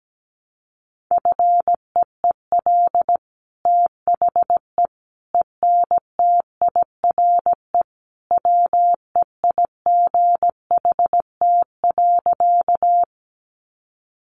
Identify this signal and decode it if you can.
Morse code, heard as a keyed tone, about 17 wpm, 700 Hz. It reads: FEEL THE ENTIRE WEIGHT.
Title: Sentences from Top 800 Words 17wpm